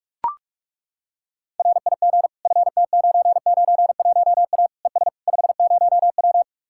Morse code, heard as a keyed tone, about 40 wpm, 700 Hz. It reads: RIG FT991A ES 50W